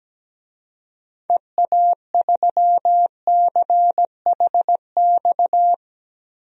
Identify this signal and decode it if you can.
Morse code, heard as a keyed tone, about 17 wpm, 700 Hz. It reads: EA3CHX
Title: DXCC-100 Call Signs - 6 Characters - Repeated 2x - 17wpm